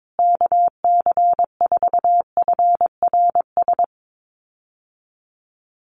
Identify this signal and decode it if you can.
Morse code, heard as a keyed tone, about 22 wpm, 700 Hz. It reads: KC4FRS
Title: USA Call Signs Repeated 3x 22wpm